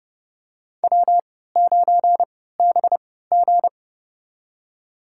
Morse code, heard as a keyed tone, about 30 wpm, spaced as for 17 wpm, 700 Hz. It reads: W9BG
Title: USA Call Signs - 4 Characters - 3x Character Spacing - 30wpm